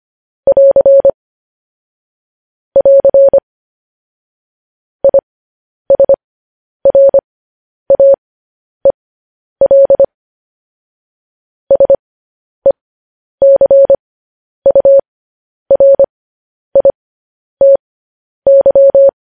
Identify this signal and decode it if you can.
Morse code, heard as a keyed tone, about 25 wpm, spaced as for 10 wpm, 550 Hz. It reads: <AR> <AR> ISRAEL SECURITY
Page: news-headlines-in-morse-code-at-10-wpm